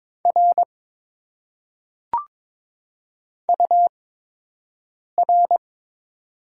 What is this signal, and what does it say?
Morse code, heard as a keyed tone, about 22 wpm, spaced as for 6 wpm, 700 Hz. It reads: R UR
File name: Top 90 QSO Elements - 8x Character Spacing - 22wpm